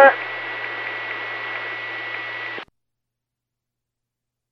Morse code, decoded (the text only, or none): none